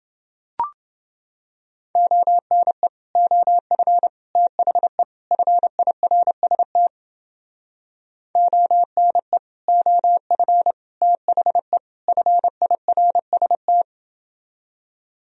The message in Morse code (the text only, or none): ONE OF THE FIRST ONE OF THE FIRST